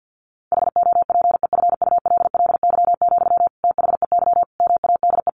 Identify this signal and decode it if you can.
Morse code, heard as a keyed tone, about 50 wpm, 700 Hz. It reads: HOPEFULLY, THEY MADE
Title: Sentences from Top 1500 Words 50wpm